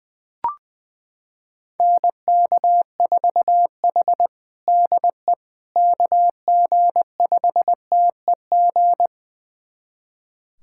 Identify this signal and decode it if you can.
Morse code, heard as a keyed tone, about 21 wpm, 700 Hz. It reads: NK4H DE KG5TEG
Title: USA Callsign DE Callsign 20wpm